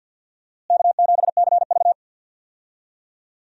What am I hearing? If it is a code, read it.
Morse code, heard as a keyed tone, about 48 wpm, 700 Hz. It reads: K7CV
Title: USA Call Signs - 4 Characters - Repeated 3x - 50wpm